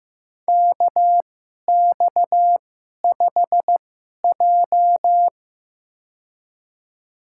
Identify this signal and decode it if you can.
Morse code, heard as a keyed tone, about 15 wpm, spaced as for 11 wpm, 700 Hz. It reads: KX5J
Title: USA Call Signs - 4 Characters - 2x Character Spacing - 15wpm